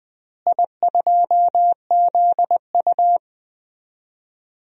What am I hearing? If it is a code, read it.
Morse code, heard as a keyed tone, about 21 wpm, 700 Hz. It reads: I2ZU
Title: DXCC-100 Call Signs - 4 Characters - Repeated 3x - 20wpm